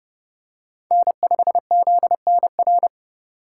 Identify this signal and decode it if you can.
Morse code, heard as a keyed tone, about 31 wpm, 700 Hz. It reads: N5ZNR
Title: USA Call Signs - 5 Characters - 3x Character Spacing - 30wpm